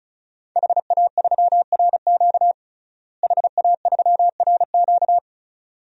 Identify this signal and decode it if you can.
Morse code, heard as a keyed tone, about 34 wpm, 700 Hz. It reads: HA3RQ HA3RQ
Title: DXCC-100 Call Signs - 5 Characters - Repeated 2x - 35wpm